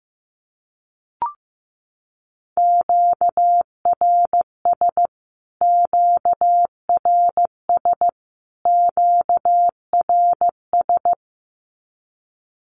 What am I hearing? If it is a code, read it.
Morse code, heard as a keyed tone, about 15 wpm, 700 Hz. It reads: QRS QRS QRS
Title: Q-Codes Repeated 3x 15wpm